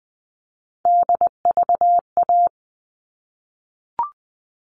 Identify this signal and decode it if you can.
Morse code, heard as a keyed tone, about 20 wpm, 700 Hz. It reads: DVA
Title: All Cut Numbers - 3-Digit 20wpm